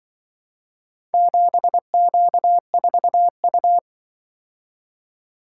Morse code, heard as a keyed tone, about 25 wpm, 700 Hz. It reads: 7Q4U